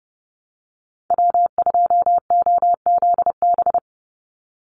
Morse code, heard as a keyed tone, about 30 wpm, 700 Hz. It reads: W2OZB
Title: Call Signs Repeated 2x - 30wpm